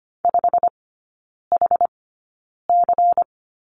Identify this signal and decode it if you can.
Morse code, heard as a keyed tone, about 25 wpm, spaced as for 9 wpm, 700 Hz. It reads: 5HC
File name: course / 25:10wpm / 093 Lesson 16e - Letter c - Three Characters 25wpm